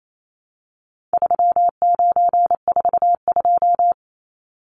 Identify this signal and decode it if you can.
Morse code, heard as a keyed tone, about 28 wpm, 700 Hz. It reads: 3942
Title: Numbers - 4-Digit 28wpm